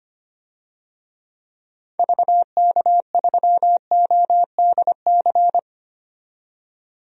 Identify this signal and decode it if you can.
Morse code, heard as a keyed tone, about 25 wpm, 700 Hz. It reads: VK3ODC
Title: DXCC-100 Call Signs - 6 Characters - 25wpm